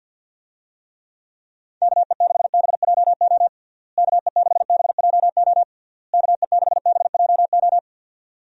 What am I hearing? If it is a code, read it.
Morse code, heard as a keyed tone, about 50 wpm, 700 Hz. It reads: KE6BJO KE6BJO KE6BJO
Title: USA Call Signs - 6 Characters - Repeated 3x - 50wpm